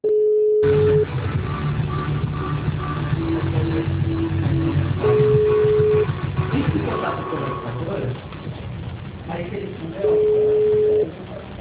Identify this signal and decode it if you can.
Ringback tone (Europe, 425 Hz), with speech.